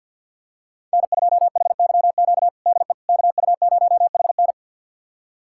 Morse code, heard as a keyed tone, about 51 wpm, 700 Hz. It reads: N2HYQ DE KU0HN
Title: USA Callsign DE Callsign - 5 Characters - 50wpm